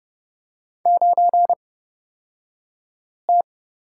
Morse code, heard as a keyed tone, about 30 wpm, spaced as for 5 wpm, 700 Hz. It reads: 9T
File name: Lesson 38d - Letter q - Two Characters 30wpm